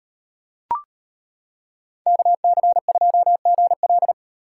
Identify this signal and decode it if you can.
Morse code, heard as a keyed tone, about 34 wpm, 700 Hz. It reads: KC2GL